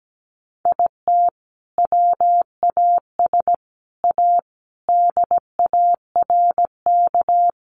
Morse code, heard as a keyed tone, about 17 wpm, 700 Hz. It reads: IT WAS A DARK
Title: Sentences from Top 500 Words 17wpm